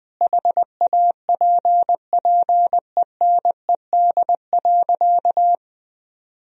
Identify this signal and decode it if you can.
Morse code, heard as a keyed tone, about 21 wpm, 700 Hz. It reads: HAPPENED.
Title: Sentences from Top 1000 Words - Review Part 4 - 20wpm